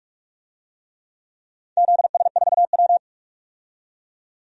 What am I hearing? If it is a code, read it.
Morse code, heard as a keyed tone, about 46 wpm, 700 Hz. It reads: ZS4W
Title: DXCC-100 Call Signs - 4 Characters - 45wpm